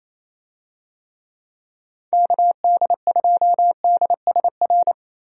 Morse code, heard as a keyed tone, about 27 wpm, 700 Hz. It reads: KD2DSR
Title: USA Call Signs - 6 Characters - 28wpm